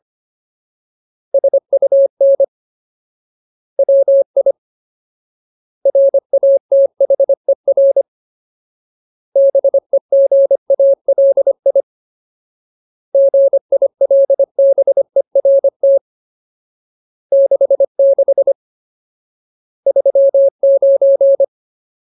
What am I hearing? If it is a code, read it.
Morse code, heard as a keyed tone, about 25 wpm, 550 Hz. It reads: SUN WI RATHER BEGALI GILBERT 66 39
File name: Ultimate Word List / 25WPM Ultimate